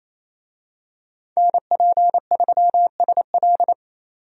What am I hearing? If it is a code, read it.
Morse code, heard as a keyed tone, about 28 wpm, 700 Hz. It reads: NP3SL
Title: USA Call Signs - 5 Characters - 28wpm